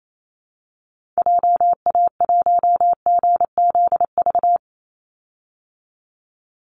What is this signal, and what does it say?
Morse code, heard as a keyed tone, about 28 wpm, 700 Hz. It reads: JA1GZV
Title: Call Signs Repeated 2x - 28wpm